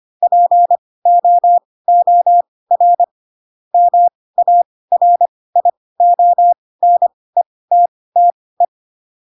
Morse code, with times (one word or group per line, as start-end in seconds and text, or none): 0.2-3.0 POOR
3.7-8.7 MARIONETTE